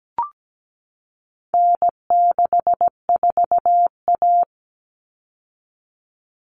Morse code, heard as a keyed tone, about 17 wpm, 700 Hz. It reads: N64A
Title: Common Cut Numbers - 4-Digit 17wpm